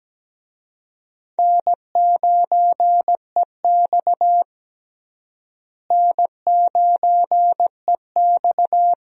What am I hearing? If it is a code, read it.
Morse code, heard as a keyed tone, about 17 wpm, 700 Hz. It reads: N9EX N9EX
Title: USA Call Signs - 4 Characters - Repeated 3x - 17wpm